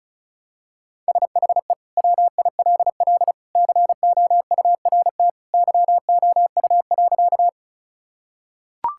Morse code, heard as a keyed tone, about 36 wpm, 700 Hz. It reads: SHE WILL COURT YOU.
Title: Sentences from Top 1600 Words 35wpm